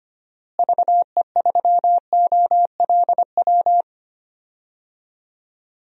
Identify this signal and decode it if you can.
Morse code, heard as a keyed tone, about 25 wpm, 700 Hz. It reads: VE3OLW
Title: DXCC-100 Call Signs - 6 Characters - Repeated 2x - 25wpm